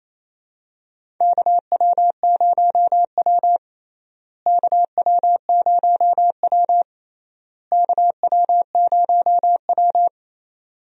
Morse code, heard as a keyed tone, about 28 wpm, 700 Hz. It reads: KW0W KW0W KW0W
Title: USA Call Signs - 4 Characters - Repeated 3x - 28wpm